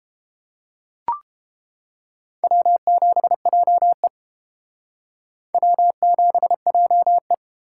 Morse code, heard as a keyed tone, about 28 wpm, 700 Hz. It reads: W7JE W7JE